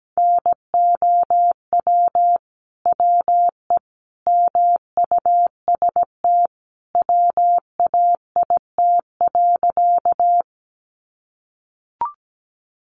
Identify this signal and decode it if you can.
Morse code, heard as a keyed tone, about 17 wpm, 700 Hz. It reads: NOW WE MUST WAIT.
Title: Sentences from Top 500 Words 17wpm